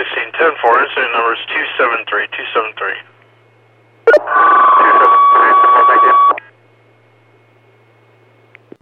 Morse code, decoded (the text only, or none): none